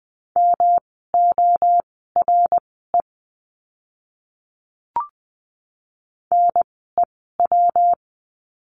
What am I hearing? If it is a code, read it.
Morse code, heard as a keyed tone, about 20 wpm, spaced as for 14 wpm, 700 Hz. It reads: MORE NEW
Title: 139 Lesson 23f - Letter f - Words 20wpm